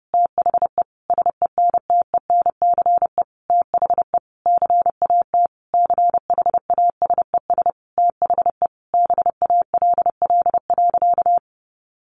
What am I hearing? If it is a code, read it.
Morse code, heard as a keyed tone, about 30 wpm, 700 Hz. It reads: THE SENTENCE THE CAT CHASES THE BALL.
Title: Additional Easy Phrases 30wpm